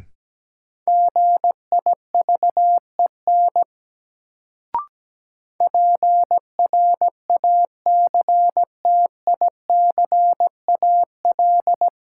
Morse code, heard as a keyed tone, about 17 wpm, 700 Hz. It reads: GIVEN PRACTICAL
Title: Sentences from Top 1400 Words - New Word Drill - 17wpm